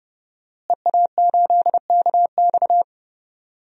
Morse code, heard as a keyed tone, about 30 wpm, 700 Hz. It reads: EA8KX
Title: DXCC-100 Call Signs - 5 Characters - Repeated 3x - 30wpm